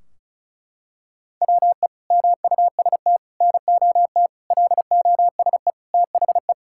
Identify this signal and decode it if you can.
Morse code, heard as a keyed tone, about 36 wpm, 700 Hz. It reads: WE MUST NOT LOSE THE
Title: Sentences from Top 1600 Words 35wpm